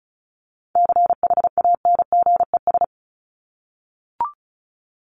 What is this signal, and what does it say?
Morse code, heard as a keyed tone, about 34 wpm, 700 Hz. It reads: CHANGES
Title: ES:S Suffix - Top 100 Words with Base Words 35wpm